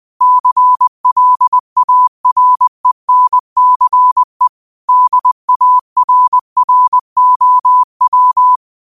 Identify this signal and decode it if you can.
Morse code, heard as a keyed tone, about 20 wpm, 1000 Hz. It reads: CLARENCE DARROW
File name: qotdfiles / 20wpm